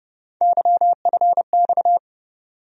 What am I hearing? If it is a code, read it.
Morse code, heard as a keyed tone, about 30 wpm, 700 Hz. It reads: YFX
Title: USA Call Sign Postfixes - 3 Letters 30wpm